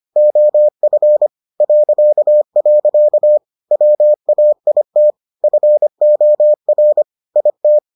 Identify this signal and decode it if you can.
Morse code, heard as a keyed tone, about 25 wpm, 600 Hz. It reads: OF .. WAIT FOR IT